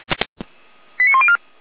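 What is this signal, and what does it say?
Five-tone selective call (ZVEI), heard as quick runs of five tones, tones of about 70 ms. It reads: 89104
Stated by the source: Five-tone ZVEI selective call received by SDR.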